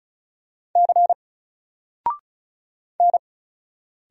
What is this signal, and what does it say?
Morse code, heard as a keyed tone, about 36 wpm, 700 Hz. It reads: C N
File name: Single-Letter-Number-35wpm